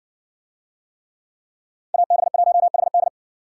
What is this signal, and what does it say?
Morse code, heard as a keyed tone, about 60 wpm, 700 Hz. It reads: AB1HD